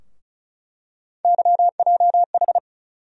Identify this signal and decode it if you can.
Morse code, heard as a keyed tone, about 36 wpm, 700 Hz. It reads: YJH
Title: USA Call Sign Postfixes - 3 Letters 35wpm